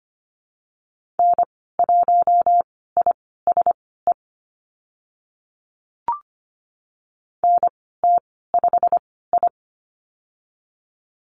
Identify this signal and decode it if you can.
Morse code, heard as a keyed tone, about 25 wpm, spaced as for 16 wpm, 690 Hz. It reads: N1ISE NT5I
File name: Lesson 15g - Number 5 - Call Signs 25wpm